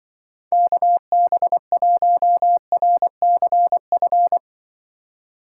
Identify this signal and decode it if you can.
Morse code, heard as a keyed tone, about 24 wpm, 700 Hz. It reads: KB1RCF
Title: Speed Racing - USA 2x3 Call Signs 20wpm